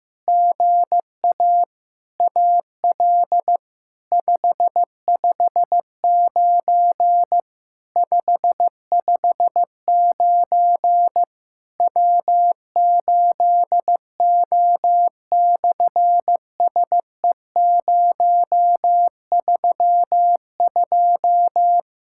Morse code, heard as a keyed tone, about 15 wpm, 700 Hz. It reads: GA AL 559 559 W8O/SE032
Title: SOTA QSO Practice 15wpm